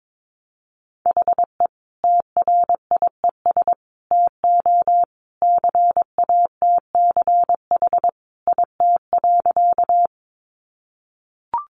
Morse code, heard as a keyed tone, about 22 wpm, 700 Hz. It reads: HE TRIES TO CATCH IT.
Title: Additional Easy Phrases 22wpm